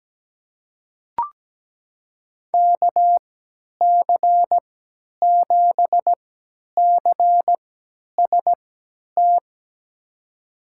Morse code, heard as a keyed tone, about 17 wpm, spaced as for 10 wpm, 700 Hz. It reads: KC7CST
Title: USA Call Signs - 6 Characters - 3x Character Spacing - 17wpm